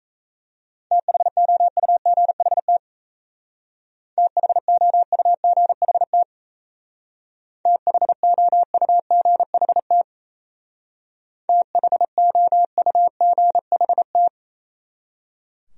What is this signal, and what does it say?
Morse code, heard as a keyed tone, about 33 wpm, 700 Hz. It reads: THOUGHT THOUGHT THOUGHT THOUGHT
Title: Speed-Racing Top 300 Words 28wpm